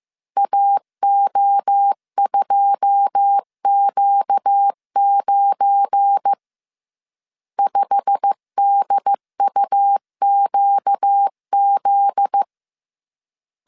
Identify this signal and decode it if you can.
Morse code, heard as a keyed tone, about 15 wpm, 800 Hz. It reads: AO2Q9 5DUQZ
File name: CW / DET / 1 9 1500C S C 14wpm 3245